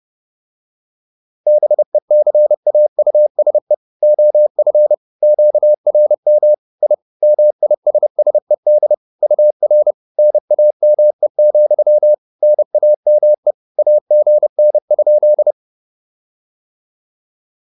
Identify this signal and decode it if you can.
Morse code, heard as a keyed tone, about 30 wpm, 600 Hz. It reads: BECAUSE OF QRM I MISSED UR NAME, NAME AGN?